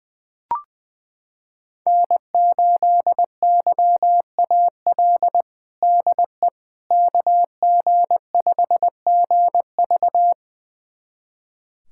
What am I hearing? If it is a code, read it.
Morse code, heard as a keyed tone, about 21 wpm, 700 Hz. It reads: N8YAL DE KG5GV
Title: USA Callsign DE Callsign - 5 Characters - 20wpm